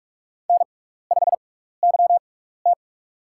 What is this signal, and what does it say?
Morse code, heard as a keyed tone, about 44 wpm, spaced as for 15 wpm, 700 Hz. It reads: N5YT
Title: USA Call Signs - 4 Characters - 6x Character Spacing - 45wpm